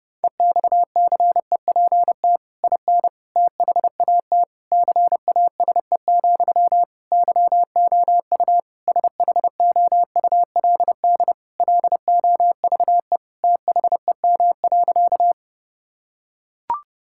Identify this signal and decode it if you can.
Morse code, heard as a keyed tone, about 30 wpm, 700 Hz. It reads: EXCEPT IN THAT CASE, YOU SHOULD LOVE THEM.
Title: Sentences from Top 1600 Words 30wpm